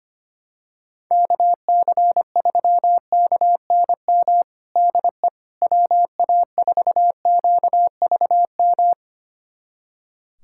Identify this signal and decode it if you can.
Morse code, heard as a keyed tone, about 25 wpm, 700 Hz. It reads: KC3KNM DE WA4QVM
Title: USA Callsign DE Callsign - 6 Characters - 25wpm